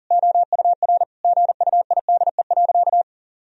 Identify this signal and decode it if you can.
Morse code, heard as a keyed tone, about 40 wpm, 700 Hz. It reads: OUR GUIDE.
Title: Sentences from Top 1000 Words - Review Part 3 - 40wpm